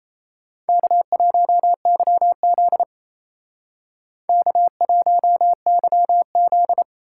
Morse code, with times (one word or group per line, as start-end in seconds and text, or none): 0.7-2.8 K1YZ
4.3-6.8 K1YZ